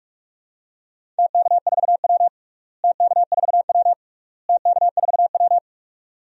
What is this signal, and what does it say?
Morse code, heard as a keyed tone, about 46 wpm, 700 Hz. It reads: TK4W TK4W TK4W
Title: DXCC-100 Call Signs - 4 Characters - Repeated 3x - 45wpm